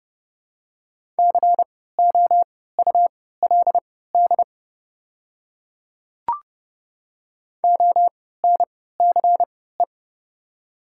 Morse code, heard as a keyed tone, about 30 wpm, spaced as for 17 wpm, 700 Hz. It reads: COULD ONCE
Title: 3x Character Spacing - Top 300 Words - 30wpm